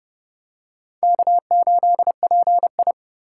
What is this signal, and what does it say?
Morse code, heard as a keyed tone, about 26 wpm, 700 Hz. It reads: K8PI